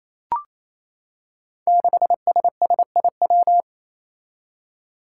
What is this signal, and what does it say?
Morse code, heard as a keyed tone, about 27 wpm, 700 Hz. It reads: 6SSIW
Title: Code Groups - 5 Letter:Numbers - 28wpm